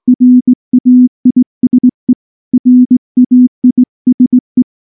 Morse code, heard as a keyed tone, about 19 wpm, 250 Hz. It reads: RAISE RAISE
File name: morse / morse-training-20WPM-1